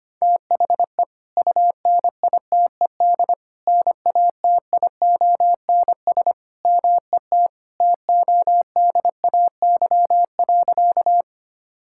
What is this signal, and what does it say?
Morse code, heard as a keyed tone, about 25 wpm, 700 Hz. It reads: THE UNITED NATIONS MET TODAY.
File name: practice / 25wpm / Sentences with 3-Word Phrases - 25wpm